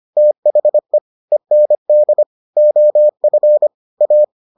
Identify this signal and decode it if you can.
Morse code, heard as a keyed tone, about 25 wpm, 600 Hz. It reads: THE END OF A